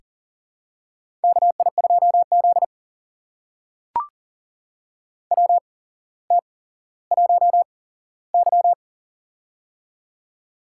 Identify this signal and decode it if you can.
Morse code, heard as a keyed tone, about 40 wpm, spaced as for 17 wpm, 700 Hz. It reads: KI2Z WT1Y
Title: USA Call Signs - 4 Characters - 8x Character Spacing - 40wpm